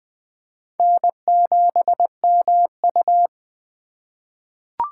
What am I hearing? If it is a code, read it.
Morse code, heard as a keyed tone, about 21 wpm, 700 Hz. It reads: N7MU
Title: USA Call Signs - 3x Character Spacing - 20wpm